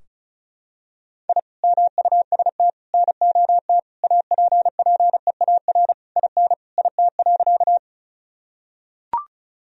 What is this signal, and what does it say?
Morse code, heard as a keyed tone, about 36 wpm, 700 Hz. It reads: I MUST NOT APPEAR IN IT.
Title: Sentences from Top 1000 Words - Review Part 2 - 35wpm